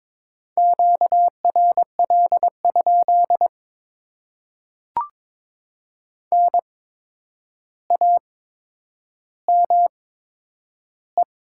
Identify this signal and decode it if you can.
Morse code, heard as a keyed tone, about 22 wpm, spaced as for 9 wpm, 700 Hz. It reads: QRL? NAME
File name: Top 90 QSO Elements - 8x Character Spacing - 22wpm